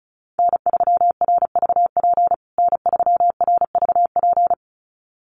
Morse code, heard as a keyed tone, about 34 wpm, 700 Hz. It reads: N3RVP N3RVP
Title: USA Call Signs Repeated 2x 35wpm